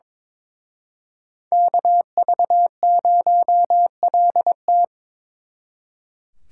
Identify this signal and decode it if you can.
Morse code, heard as a keyed tone, about 26 wpm, 700 Hz. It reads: KV0LT
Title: Speed Racing - USA Call Signs 22wpm